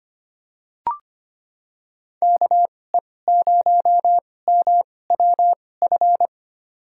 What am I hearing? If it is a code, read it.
Morse code, heard as a keyed tone, about 25 wpm, spaced as for 18 wpm, 700 Hz. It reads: KE0MWF
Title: USA Call Signs - 6 Characters - 2x Character Spacing - 25wpm